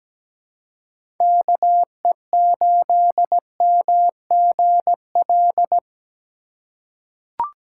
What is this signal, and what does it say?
Morse code, heard as a keyed tone, about 17 wpm, 700 Hz. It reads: KE8MGL
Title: USA Call Signs - 6 Characters - 2x Character Spacing - 17wpm